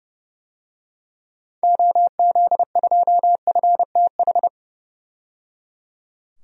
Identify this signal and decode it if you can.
Morse code, heard as a keyed tone, about 30 wpm, 700 Hz. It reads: OZ2FTH
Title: DXCC-100 Call Signs - 6 Characters - 30wpm